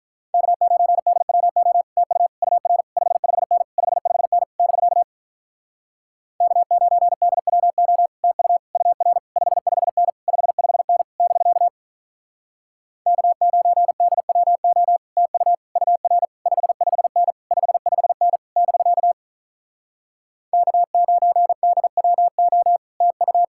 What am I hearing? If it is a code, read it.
Morse code, heard as a keyed tone, about 48 wpm, 700 Hz. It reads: K9DWO TU UR 55N 55N <BK> K9DWO TU UR 55N 55N <BK> K9DWO TU UR 55N 55N <BK> K9DWO TU